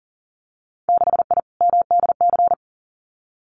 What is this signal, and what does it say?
Morse code, heard as a keyed tone, about 42 wpm, 700 Hz. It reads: 6I MDC
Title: Winter Field Day Exchanges Encore 40wpm